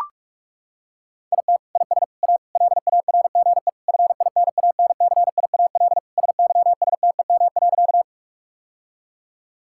Morse code, heard as a keyed tone, about 42 wpm, 700 Hz. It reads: IT IS A LARGE FINANCIAL SYSTEM.
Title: Sentences from Top 1000 Words - Review Part 5 - 45wpm